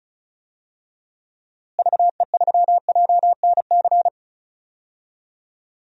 Morse code, heard as a keyed tone, about 34 wpm, 700 Hz. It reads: VE3JNC
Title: DXCC-100 Call Signs - 6 Characters - Repeated 2x - 35wpm